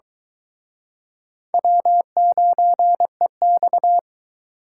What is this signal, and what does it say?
Morse code, heard as a keyed tone, about 23 wpm, 700 Hz. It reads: W9EX